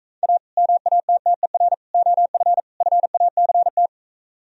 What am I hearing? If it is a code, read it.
Morse code, heard as a keyed tone, about 45 wpm, 700 Hz. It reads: A MATTER OF FACT